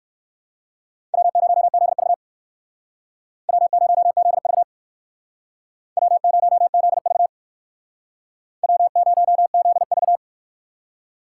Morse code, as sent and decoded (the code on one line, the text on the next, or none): .-- ----- --.. ...- / .-- ----- --.. ...- / .-- ----- --.. ...- / .-- ----- --.. ...-
W0ZV W0ZV W0ZV W0ZV